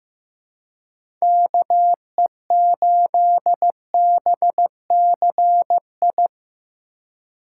Morse code, heard as a keyed tone, about 15 wpm, 700 Hz. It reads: KE8BCI
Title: USA Call Signs - 6 Characters - 4x Character Spacing - 15wpm